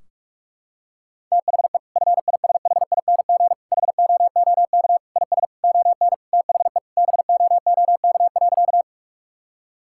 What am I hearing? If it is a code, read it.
Morse code, heard as a keyed tone, about 42 wpm, 700 Hz. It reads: THE FISHING HOOK IS ON THE BOOK.
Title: Sentences from Top 1600 Words 45wpm